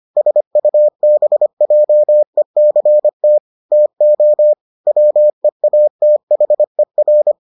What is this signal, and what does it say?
Morse code, heard as a keyed tone, about 25 wpm, 600 Hz. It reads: SUBJECT TO WEATHER